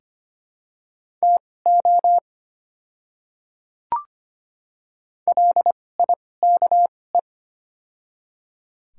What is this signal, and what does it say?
Morse code, heard as a keyed tone, about 25 wpm, spaced as for 18 wpm, 700 Hz. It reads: TO LIKE
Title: 2x Character Spacing - Top 200 Words - 25wpm